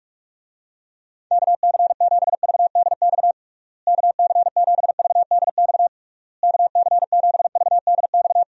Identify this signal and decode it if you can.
Morse code, heard as a keyed tone, about 43 wpm, 700 Hz. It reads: KC7VDX KC7VDX KC7VDX
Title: USA Call Signs - 6 Characters - Repeated 3x - 45wpm